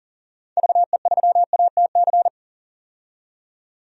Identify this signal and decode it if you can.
Morse code, heard as a keyed tone, about 40 wpm, 700 Hz. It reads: VE3ATC
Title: DXCC-100 Call Signs - 6 Characters - 40wpm